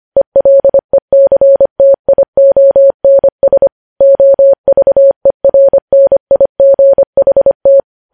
Morse code, heard as a keyed tone, about 25 wpm, 550 Hz. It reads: ELECTIONS OVERNIGHT
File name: podcast / News headlines-1x-25x25